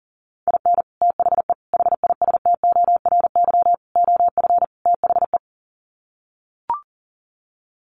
Morse code, heard as a keyed tone, about 40 wpm, 700 Hz. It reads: IN THE HISTORY OF THE
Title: Sets of 5 Words 40wpm